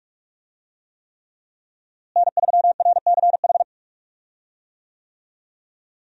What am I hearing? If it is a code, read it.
Morse code, heard as a keyed tone, about 46 wpm, 700 Hz. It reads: N3RCH
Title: USA Call Signs - 5 Characters - Repeated 3x - 45wpm